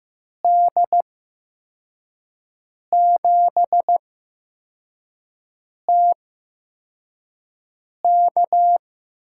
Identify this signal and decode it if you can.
Morse code, heard as a keyed tone, about 15 wpm, spaced as for 4 wpm, 700 Hz. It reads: D7TK
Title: USA Call Signs - 5 Characters - 8x Character Spacing - 15wpm